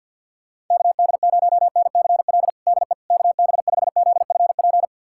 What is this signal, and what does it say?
Morse code, heard as a keyed tone, about 46 wpm, 700 Hz. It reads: KD0NCL DE KB5ZFP
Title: USA Callsign DE Callsign - 6 Characters - 50wpm